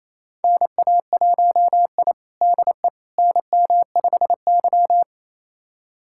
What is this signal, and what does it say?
Morse code, heard as a keyed tone, about 28 wpm, 700 Hz. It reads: NA1I DE NM5Y
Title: USA Callsign DE Callsign - 4 Characters - 28wpm